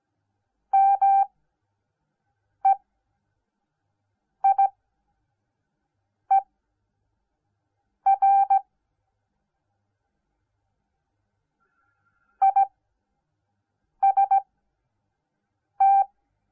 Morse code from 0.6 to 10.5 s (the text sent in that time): MEIER